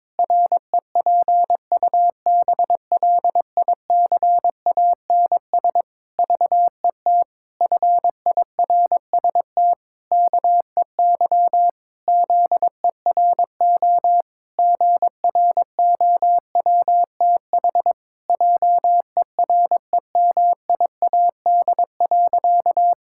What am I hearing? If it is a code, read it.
Morse code, heard as a keyed tone, about 22 wpm, 700 Hz. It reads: REPUBLICANS VET FIRST KEY ZERO GROWTH JEREMIAD.
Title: Pangram Challenge v2 - 22wpm